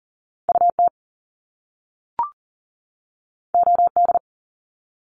Morse code, heard as a keyed tone, about 38 wpm, 700 Hz. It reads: UT OD
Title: Top Two Letter Combinations in Words 40wpm